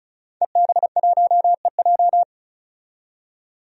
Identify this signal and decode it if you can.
Morse code, heard as a keyed tone, about 34 wpm, 700 Hz. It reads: EB1EJ
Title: DXCC-100 Call Signs - 5 Characters - 35wpm